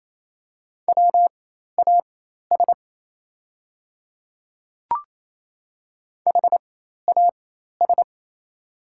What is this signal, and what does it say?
Morse code, heard as a keyed tone, about 28 wpm, spaced as for 13 wpm, 700 Hz. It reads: WAS HAS
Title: 4x Character Spacing - Top 100 Words - 28wpm